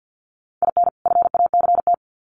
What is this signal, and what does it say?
Morse code, heard as a keyed tone, about 50 wpm, 700 Hz. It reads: IN FACT
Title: Sets of 2 Words 50wpm